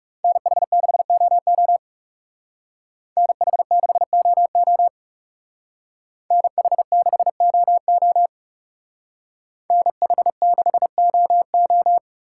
Morse code, heard as a keyed tone, about 38 wpm, 700 Hz. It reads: NH6OO NH6OO NH6OO NH6OO